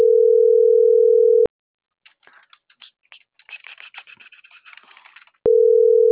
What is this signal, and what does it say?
Ringback tone (North America), 440 + 480 Hz.